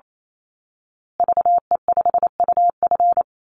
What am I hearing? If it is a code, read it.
Morse code, heard as a keyed tone, about 28 wpm, 700 Hz. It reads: VE5UF
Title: Call Signs Repeated 2x - 28wpm